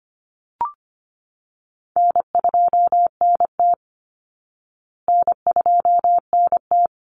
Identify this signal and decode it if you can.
Morse code, heard as a keyed tone, about 25 wpm, 700 Hz. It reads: N2NT N2NT
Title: Easy Call Signs Repeated 2x - 25wpm